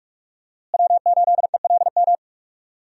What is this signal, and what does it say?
Morse code, heard as a keyed tone, about 44 wpm, 700 Hz. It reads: W8ELM
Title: USA Call Signs - 5 Characters - Repeated 2x - 45wpm